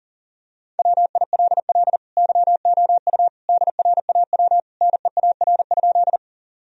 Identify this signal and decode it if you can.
Morse code, heard as a keyed tone, about 40 wpm, 700 Hz. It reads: WILL YOU DRAW NEAR?
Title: Sentences from Top 1000 Words - Review Part 1 - 40wpm